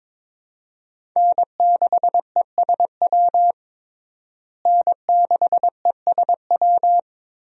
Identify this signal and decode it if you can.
Morse code, heard as a keyed tone, about 22 wpm, 700 Hz. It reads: N6ESW N6ESW
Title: USA Call Signs - 5 Characters - Repeated 2x - 22wpm